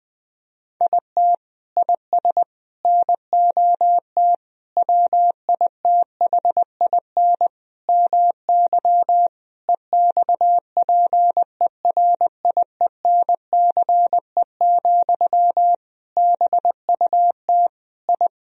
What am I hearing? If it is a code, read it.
Morse code, heard as a keyed tone, about 20 wpm, 700 Hz. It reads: IT IS NOT WITHIN MY EXPERIENCE, BUT I
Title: Sentences from Top 1000 Words - Review Part 4 - 20wpm